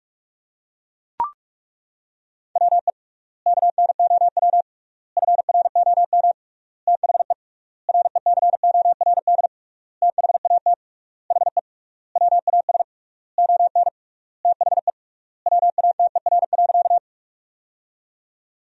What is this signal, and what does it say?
Morse code, heard as a keyed tone, about 42 wpm, 700 Hz. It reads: WE KNOW FROM THE RECORD THAT HE WAS ON THE WATER.
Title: 3x Word-Spacing - Sentences from Top 500 Words - 45wpm